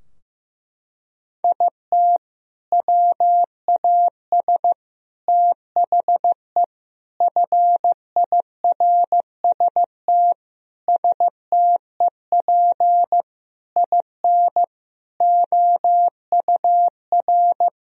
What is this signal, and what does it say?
Morse code, heard as a keyed tone, about 15 wpm, 700 Hz. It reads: IT WAS THE FIRST STEP IN OUR
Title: Sentences from Top 1000 Words - Review Part 4 - 15wpm